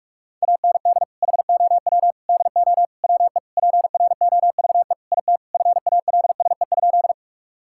Morse code, heard as a keyed tone, about 46 wpm, 700 Hz. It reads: AND HOW DO WE PROVE IT FALSE?
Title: Sentences from Top 1600 Words 45wpm